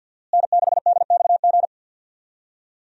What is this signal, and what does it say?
Morse code, heard as a keyed tone, about 50 wpm, 700 Hz. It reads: N6DXG
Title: USA Call Signs - 5 Characters - Repeated 3x - 50wpm